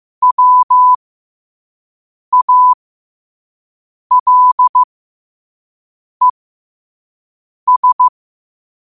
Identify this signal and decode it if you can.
Morse code, heard as a keyed tone, about 15 wpm, spaced as for 5 wpm, 1000 Hz. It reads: WALES
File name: 5wpm